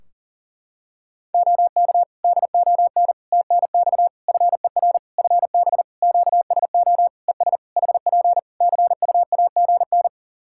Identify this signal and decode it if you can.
Morse code, heard as a keyed tone, about 39 wpm, 700 Hz. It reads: OK DON TNX FER FB QSO ES HP CUAGN
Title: Practice QSO - Exchange 4 - Ending - 40wpm